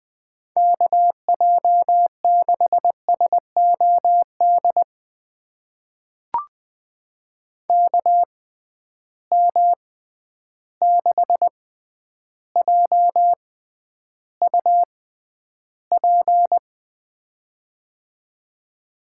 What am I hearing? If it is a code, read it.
Morse code, heard as a keyed tone, about 20 wpm, spaced as for 10 wpm, 700 Hz. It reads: KJ6SOD KM6JUP
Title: USA Call Signs - 6 Characters - 6x Character Spacing - 20wpm